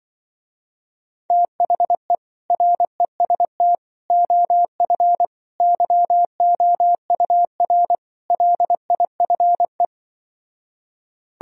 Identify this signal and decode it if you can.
Morse code, heard as a keyed tone, about 25 wpm, 700 Hz. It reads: THE REST OF YOUR LIFE